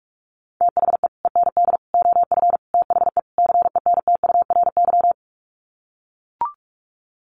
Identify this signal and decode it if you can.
Morse code, heard as a keyed tone, about 44 wpm, 700 Hz. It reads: THE END OF THE CENTURY
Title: Sets of 5 Words Encore2 45wpm